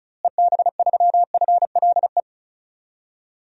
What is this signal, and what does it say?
Morse code, heard as a keyed tone, about 36 wpm, 700 Hz. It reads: EB3FLE
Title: DXCC-100 Call Signs - 6 Characters - Repeated 2x - 35wpm